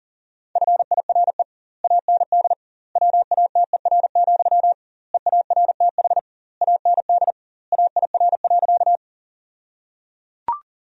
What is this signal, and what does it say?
Morse code, heard as a keyed tone, about 37 wpm, 700 Hz. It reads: FIRE AND WATER, EARTH AND AIR.
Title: 2x Word-Spacing - Sentences from Top 400 Words - 40wpm